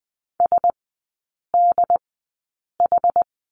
Morse code, heard as a keyed tone, about 20 wpm, spaced as for 8 wpm, 700 Hz. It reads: SDH
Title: 114 Lesson 19e - Letter w - Three Characters 20wpm